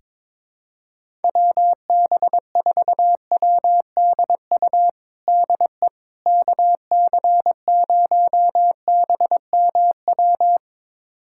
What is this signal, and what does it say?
Morse code, heard as a keyed tone, about 22 wpm, 700 Hz. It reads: WB4WDU DE KC0BMW
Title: USA Callsign DE Callsign - 6 Characters - 22wpm